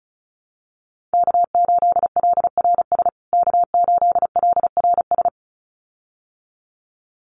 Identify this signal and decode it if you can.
Morse code, heard as a keyed tone, about 34 wpm, 700 Hz. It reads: K8LRS K8LRS
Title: USA Call Signs Repeated 2x 35wpm